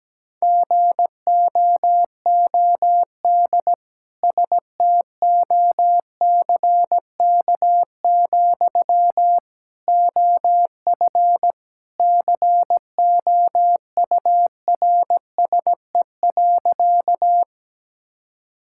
Morse code, heard as a keyed tone, about 17 wpm, 700 Hz. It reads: GOOD STOCK, OF COURSE.
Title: Sentences from Top 1000 Words - Review Part 5 - 17wpm